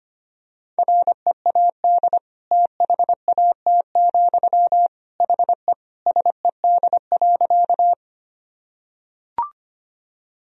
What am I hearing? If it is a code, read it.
Morse code, heard as a keyed tone, about 25 wpm, 700 Hz. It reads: READ THAT, HE SED.
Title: Sentences from Top 200 Words and Abbreviations 25wpm